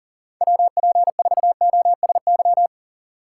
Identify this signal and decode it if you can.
Morse code, heard as a keyed tone, about 51 wpm, 700 Hz. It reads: WP4OSY